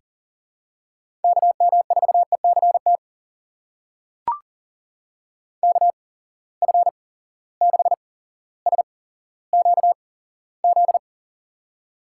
Morse code, heard as a keyed tone, about 38 wpm, spaced as for 17 wpm, 700 Hz. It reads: KM4ECT KF6SQZ
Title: USA Call Signs - 6 Characters - 8x Character Spacing - 40wpm